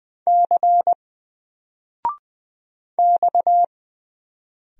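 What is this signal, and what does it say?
Morse code, heard as a keyed tone, about 20 wpm, 700 Hz. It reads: C X
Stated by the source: Single letter and number Morse drill at 20 wpm, from N4LCD.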